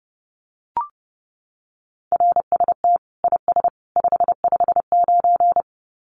Morse code, heard as a keyed tone, about 30 wpm, 700 Hz. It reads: RST IS 559
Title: Common QSO Phrases 30wpm